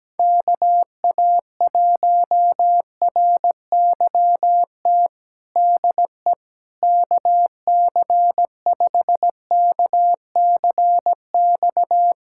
Morse code, heard as a keyed tone, about 17 wpm, 700 Hz. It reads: KA1RYT DE KC5KCX
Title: USA Callsign DE Callsign - 6 Characters - 17wpm